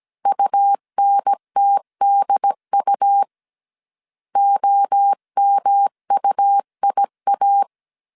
Morse code, heard as a keyed tone, about 17 wpm, 800 Hz. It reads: UNTDU OMUIA
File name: CW / DET / 1 3 1500C S C 16wpm 2148